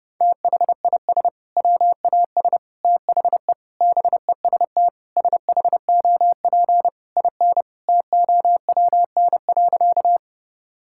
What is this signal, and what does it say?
Morse code, heard as a keyed tone, about 30 wpm, 700 Hz. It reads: THIS WAS THE BEST SHOP IN TOWN.
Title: Sentences from Top 1600 Words 30wpm